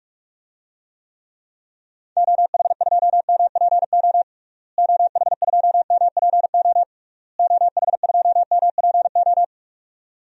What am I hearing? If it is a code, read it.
Morse code, heard as a keyed tone, about 45 wpm, 700 Hz. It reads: OH2MPO OH2MPO OH2MPO
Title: DXCC-100 Call Signs - 6 Characters - Repeated 3x - 45wpm